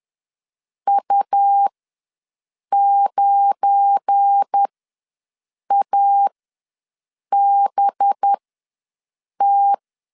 Morse code, heard as a keyed tone, about 10 wpm, spaced as for 6 wpm, 800 Hz. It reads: U9ABT
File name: LEZIONI / 10 SIGNS / 10C DOT ws 10wpm 4213